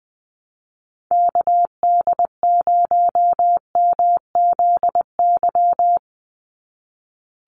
Morse code, heard as a keyed tone, about 20 wpm, 700 Hz. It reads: KD0MZY
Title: USA Call Signs 20wpm